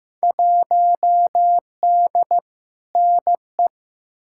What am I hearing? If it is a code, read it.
Morse code, heard as a keyed tone, about 15 wpm, 700 Hz. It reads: 1D NE
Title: Field Day Exchanges - 15wpm